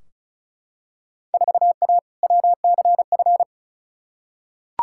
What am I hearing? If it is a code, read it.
Morse code, heard as a keyed tone, about 34 wpm, 700 Hz. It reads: 4A WCF
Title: Field Day Exchanges - 35wpm